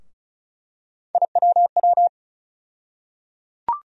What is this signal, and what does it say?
Morse code, heard as a keyed tone, about 36 wpm, 700 Hz. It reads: IWW
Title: USA Call Sign Postfixes - 3 Letters 35wpm